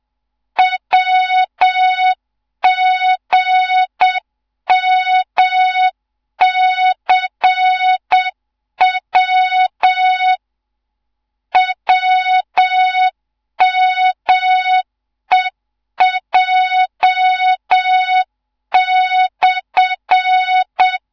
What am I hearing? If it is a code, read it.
Morse code, heard as a keyed tone, about 7 wpm, 3700 Hz, with noise. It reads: WGMCW WMEJ/
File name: Morze-HA2MN / hang / 14 280 HA2MN